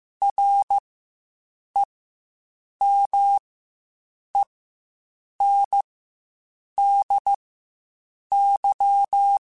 Morse code, heard as a keyed tone, about 15 wpm, spaced as for 7 wpm, 800 Hz. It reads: REMENDY
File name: Quotes Spaced / Quotes Spaced-mp3s / Temper sp @10